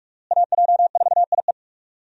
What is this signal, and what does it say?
Morse code, heard as a keyed tone, about 42 wpm, 700 Hz. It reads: AJ4IE
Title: USA Call Signs - 5 Characters - Repeated 3x - 45wpm